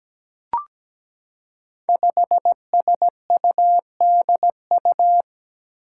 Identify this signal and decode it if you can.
Morse code, heard as a keyed tone, about 17 wpm, 700 Hz. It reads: 5SUDU
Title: Five Challenging Letter-Number Combinations 17wpm